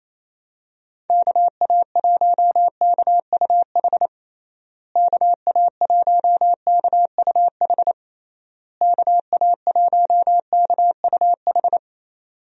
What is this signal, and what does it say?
Morse code, heard as a keyed tone, about 28 wpm, 700 Hz. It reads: KA1KUH KA1KUH KA1KUH
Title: USA Call Signs - 6 Characters - Repeated 3x - 28wpm